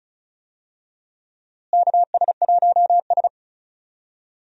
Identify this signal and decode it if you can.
Morse code, heard as a keyed tone, about 34 wpm, 700 Hz. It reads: KS1S
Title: USA Call Signs - 4 Characters - Repeated 2x - 35wpm